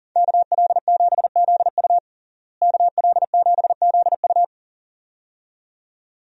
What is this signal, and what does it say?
Morse code, heard as a keyed tone, about 40 wpm, 700 Hz. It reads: KL7ZU KL7ZU
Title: USA Call Signs - 5 Characters - Repeated 2x - 40wpm